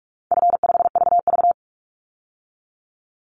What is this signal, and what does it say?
Morse code, heard as a keyed tone, about 44 wpm, 700 Hz. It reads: F5VV
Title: Easy Call Signs - 45wpm